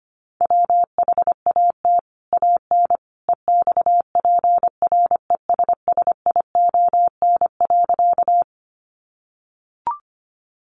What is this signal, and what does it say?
Morse code, heard as a keyed tone, about 25 wpm, 700 Hz. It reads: WHAT AN EXPRESSION.
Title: Sentences from Top 1200 Words 25wpm